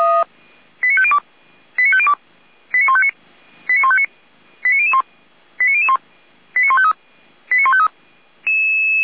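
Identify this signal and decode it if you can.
Five-tone selective call (ZVEI), heard as quick runs of five tones, tones of about 70 ms. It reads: 89502 89502 89159 89159 89001 89001 89153 89153
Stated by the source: ZVEI five-tone selective call received on an SDR.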